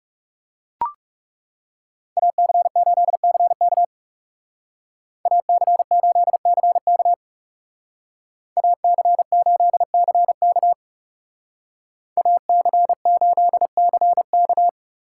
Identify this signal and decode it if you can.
Morse code, heard as a keyed tone, about 39 wpm, 700 Hz. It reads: AC8CK AC8CK AC8CK AC8CK